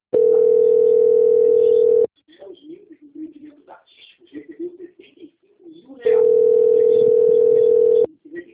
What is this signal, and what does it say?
Ringback tone (North America), 440 + 480 Hz, with speech.
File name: out- - - - - .3614